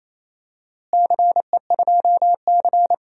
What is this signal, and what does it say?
Morse code, heard as a keyed tone, about 28 wpm, 700 Hz. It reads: CE2C
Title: DXCC-100 Call Signs - 4 Characters - Repeated 2x - 28wpm